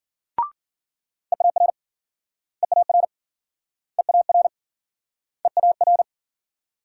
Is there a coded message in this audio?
Yes: Morse code from 1.3 to 6.0 s, heard as a keyed tone, about 53 wpm, 700 Hz. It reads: EAR EAR EAR EAR